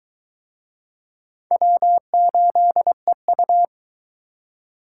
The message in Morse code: W8EU

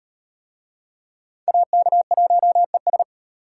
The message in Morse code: AK1ES

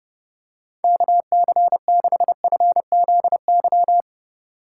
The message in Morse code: KC6FZY

